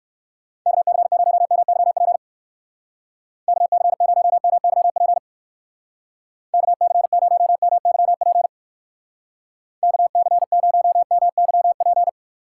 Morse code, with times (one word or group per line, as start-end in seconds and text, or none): 0.7-2.2 KC0MYP
3.5-5.2 KC0MYP
6.5-8.5 KC0MYP
9.8-12.1 KC0MYP